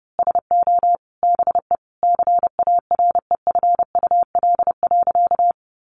Morse code, heard as a keyed tone, about 30 wpm, 700 Hz. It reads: SO BE CAREFUL.
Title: A Sound Of Thunder 30wpm